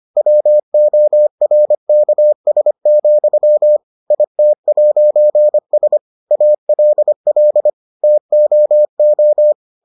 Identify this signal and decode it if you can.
Morse code, heard as a keyed tone, about 25 wpm, 600 Hz. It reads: WORKS, IT'S ALL TOO